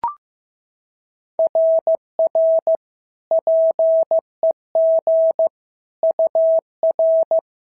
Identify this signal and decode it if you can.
Morse code, heard as a keyed tone, about 14 wpm, 650 Hz. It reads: RR PEG UR
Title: SOTA QSO Practice 15wpm